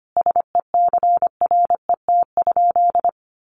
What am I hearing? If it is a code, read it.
Morse code, heard as a keyed tone, about 25 wpm, 700 Hz. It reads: SECRET?
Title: Sentences from Top 1200 Words 25wpm